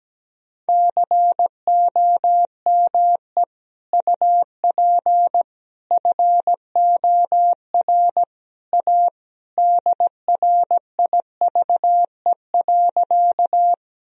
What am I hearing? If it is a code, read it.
Morse code, heard as a keyed tone, about 17 wpm, 700 Hz. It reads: COME UP FOR A DRIVE.
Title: Sentences from Top 1000 Words - Review Part 2 - 17wpm